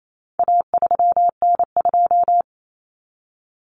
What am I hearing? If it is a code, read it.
Morse code, heard as a keyed tone, about 28 wpm, 700 Hz. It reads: A3N2
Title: Common Cut Numbers - 4-Digit 28wpm